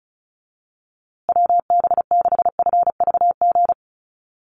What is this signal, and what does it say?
Morse code, heard as a keyed tone, about 34 wpm, 700 Hz. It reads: WB6FVG
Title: USA Call Signs Repeated 3x 35wpm